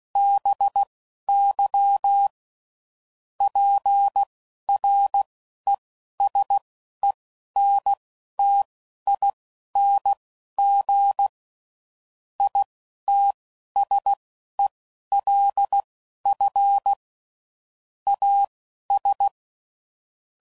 Morse code, heard as a keyed tone, about 16 wpm, spaced as for 11 wpm, 800 Hz. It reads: BY PRESENTING ITSELF AS
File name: CW Code Groups @10WPM / Technical Subjects / XC